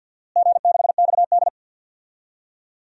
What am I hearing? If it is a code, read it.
Morse code, heard as a keyed tone, about 51 wpm, 700 Hz. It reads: G6XD
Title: DXCC-100 Call Signs - 4 Characters - Repeated 3x - 50wpm